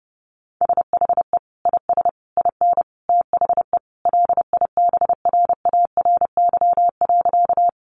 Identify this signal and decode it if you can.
Morse code, heard as a keyed tone, about 28 wpm, 700 Hz. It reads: SHE IS IN THE LIBRARY.
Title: Sentences from Top 600 Words 30wpm